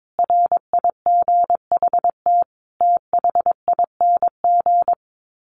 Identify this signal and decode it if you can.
Morse code, heard as a keyed tone, about 22 wpm, 700 Hz. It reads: RIGHT THING
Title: Sets of 5 Words 22wpm